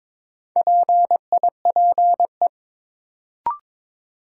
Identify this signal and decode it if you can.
Morse code, heard as a keyed tone, about 22 wpm, 700 Hz. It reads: PIPE
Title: Sentences from Top 1500 Words - New Word Drill - 22wpm